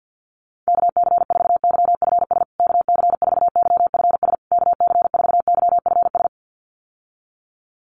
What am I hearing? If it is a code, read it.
Morse code, heard as a keyed tone, about 50 wpm, 700 Hz. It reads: KC4YFS KC4YFS KC4YFS
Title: USA Call Signs Repeated 3x 50wpm